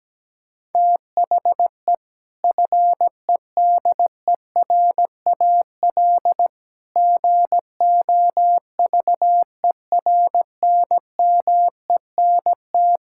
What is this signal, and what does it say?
Morse code, heard as a keyed tone, about 17 wpm, 700 Hz. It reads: THE FEDERAL GOVERNMENT
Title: Sentences from Top 1000 Words - Review Part 5 - 17wpm